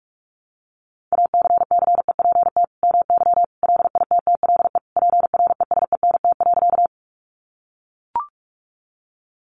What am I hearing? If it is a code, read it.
Morse code, heard as a keyed tone, about 44 wpm, 700 Hz. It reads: ACCEPT MY LITTLE PRESENT.
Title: Sentences from Top 1300 Words 45wpm